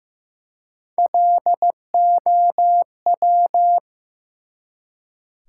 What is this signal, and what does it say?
Morse code, heard as a keyed tone, about 15 wpm, 700 Hz. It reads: LOW
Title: Three Letter Words - 15wpm